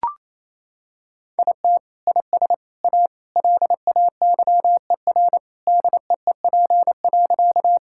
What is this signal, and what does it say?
Morse code, heard as a keyed tone, about 28 wpm, 700 Hz. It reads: IT IS A LAYER DEEP.
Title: Sentences from Top 1600 Words 28wpm